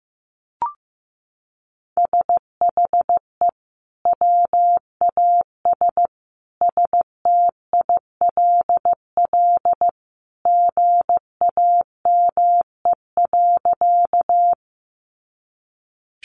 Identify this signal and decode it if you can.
Morse code, heard as a keyed tone, about 15 wpm, 700 Hz. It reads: SHE WAS STILL GAME.
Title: Sentences from Top 500 Words 15wpm